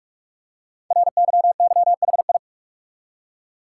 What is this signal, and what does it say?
Morse code, heard as a keyed tone, about 44 wpm, 700 Hz. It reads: RYYHI
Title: Code Groups - 5 Letters - 45wpm